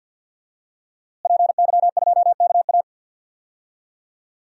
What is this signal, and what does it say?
Morse code, heard as a keyed tone, about 52 wpm, 700 Hz. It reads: PY2KA
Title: DXCC-100 Call Signs - 5 Characters - Repeated 2x - 50wpm